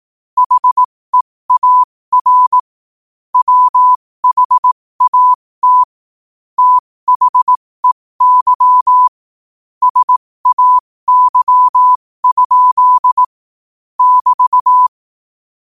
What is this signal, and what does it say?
Morse code, heard as a keyed tone, about 18 wpm, spaced as for 15 wpm, 1000 Hz. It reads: HEAR WHAT THEY SAY? <BT>
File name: qotdfiles / 15wpm